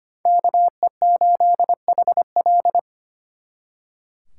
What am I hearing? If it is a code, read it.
Morse code, heard as a keyed tone, about 32 wpm, 700 Hz. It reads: KE8HL